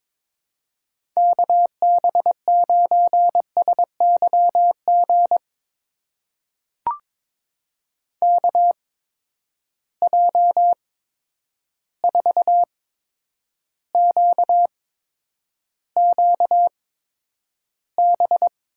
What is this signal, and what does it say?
Morse code, heard as a keyed tone, about 22 wpm, spaced as for 9 wpm, 700 Hz. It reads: KB9SYG KJ4QQB
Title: USA Call Signs - 6 Characters - 8x Character Spacing - 22wpm